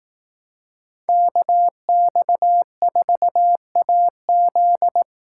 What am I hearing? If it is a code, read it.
Morse code, heard as a keyed tone, about 18 wpm, 700 Hz. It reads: KX4AZ
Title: Speed Racing - USA 2x2 Call Signs 15wpm